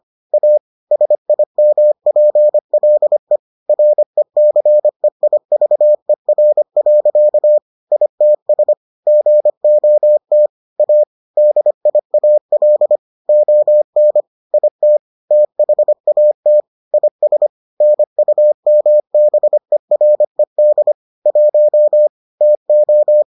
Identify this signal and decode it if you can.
Morse code, heard as a keyed tone, about 25 wpm, 600 Hz. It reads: A SIMPLE RECEIVER. ITS GOT A DIAL ON IT THAT IS NUMBERED 1 TO